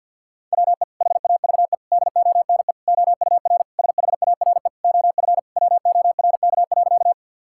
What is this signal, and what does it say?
Morse code, heard as a keyed tone, about 50 wpm, 700 Hz. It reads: WE HAVE DONE OUR SHARE OF WORK.
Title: Sentences from Top 1000 Words - Review Part 6 - 50wpm